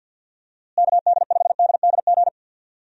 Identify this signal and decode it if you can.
Morse code, heard as a keyed tone, about 52 wpm, 700 Hz. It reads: KD5DDG
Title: USA Call Signs - 6 Characters - Repeated 2x - 50wpm